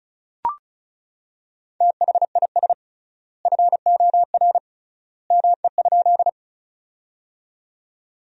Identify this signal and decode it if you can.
Morse code, heard as a keyed tone, about 34 wpm, 700 Hz. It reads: THIS FOR ME?
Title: 3x Word-Spacing - Sentences from Top 200 Words - 35wpm